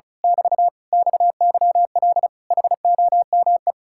Morse code, heard as a keyed tone, about 33 wpm, 700 Hz. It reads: <BT> XYL HOME
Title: Speed-Racing Common QSO Phrases - 35wpm